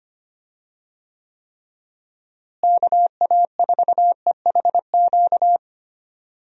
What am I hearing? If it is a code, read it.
Morse code, heard as a keyed tone, about 25 wpm, 700 Hz. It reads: KA4EHQ
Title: USA Call Signs - 6 Characters - Repeated 3x - 25wpm